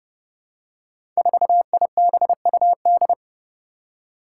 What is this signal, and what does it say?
Morse code, heard as a keyed tone, about 30 wpm, 700 Hz. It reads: 4IBUD
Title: Five Challenging Letter-Number Combinations 30wpm